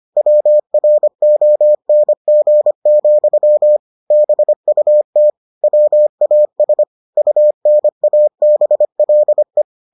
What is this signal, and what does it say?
Morse code, heard as a keyed tone, about 25 wpm, 600 Hz. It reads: WRONG, BUT WAS UNABLE